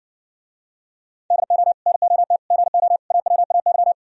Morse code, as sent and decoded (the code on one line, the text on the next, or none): -.. --- / -. --- - / --. --- / .- .-- .- -.--
DO NOT GO AWAY